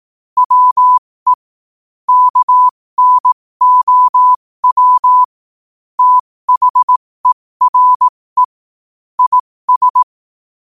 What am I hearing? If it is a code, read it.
Morse code, heard as a keyed tone, about 18 wpm, spaced as for 15 wpm, 1000 Hz. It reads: WE KNOW THERE IS
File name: qotdfiles / 15wpm